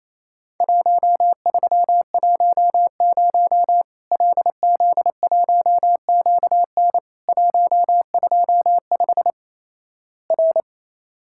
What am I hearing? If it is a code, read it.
Morse code, heard as a keyed tone, about 28 wpm, 700 Hz. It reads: 1310 LZ1QN 125 R
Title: Worked All Europe DX Contest - QTC Exchange - 28wpm